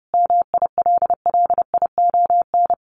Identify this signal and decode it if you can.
Morse code, heard as a keyed tone, about 30 wpm, 700 Hz. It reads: MILLION
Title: Sets of 5 Words Encore2 30wpm